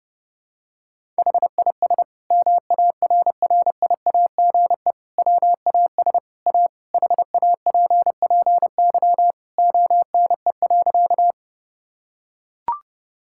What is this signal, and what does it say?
Morse code, heard as a keyed tone, about 30 wpm, 700 Hz. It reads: HIS MARRIAGE WAS A HAPPY ONE.
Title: Sentences from Top 1000 Words - Review Part 5 - 30wpm